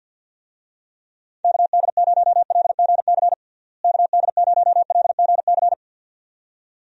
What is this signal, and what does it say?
Morse code, heard as a keyed tone, about 51 wpm, 700 Hz. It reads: KD0LGC KD0LGC
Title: USA Call Signs - 6 Characters - Repeated 2x - 50wpm